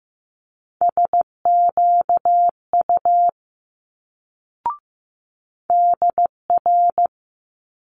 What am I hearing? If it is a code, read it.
Morse code, heard as a keyed tone, about 15 wpm, 700 Hz. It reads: SQU DR
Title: Multi-letter Phonemes Quick Fire 15wpm